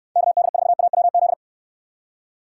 Morse code, heard as a keyed tone, about 68 wpm, 700 Hz. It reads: KD4AWZ